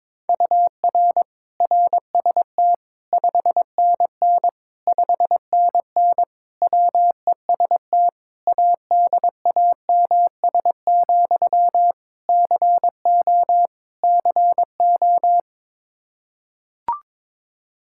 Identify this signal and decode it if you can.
Morse code, heard as a keyed tone, about 22 wpm, 700 Hz. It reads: UR RST 5NN 5NN WEST ADAMS, CO CO
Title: Practice QSO - Exchange 1 - Beginning - 22wpm